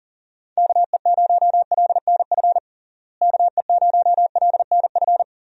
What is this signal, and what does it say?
Morse code, heard as a keyed tone, about 40 wpm, 700 Hz. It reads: KE0LNF KE0LNF
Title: USA Call Signs - 6 Characters - Repeated 3x - 40wpm